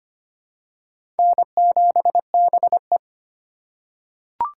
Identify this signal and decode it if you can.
Morse code, heard as a keyed tone, about 25 wpm, 700 Hz. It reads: N7BE
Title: USA Call Signs - 2x Character Spacing - 25wpm